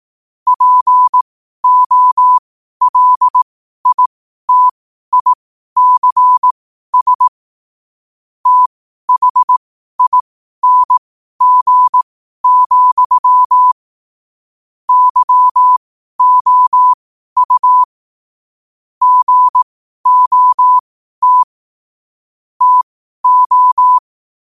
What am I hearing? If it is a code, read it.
Morse code, heard as a keyed tone, about 18 wpm, spaced as for 12 wpm, 1000 Hz. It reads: POLITICS THING, YOU GOT TO
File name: qotdfiles / 12wpm